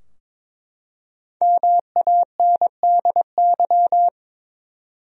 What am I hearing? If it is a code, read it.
Morse code, heard as a keyed tone, about 22 wpm, 700 Hz. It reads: MANDY
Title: USA Common Names - Max 5 Letters - 22wpm